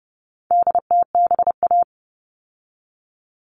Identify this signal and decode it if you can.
Morse code, heard as a keyed tone, about 30 wpm, 700 Hz. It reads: DTBA
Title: All Cut Numbers - 4-Digit 30wpm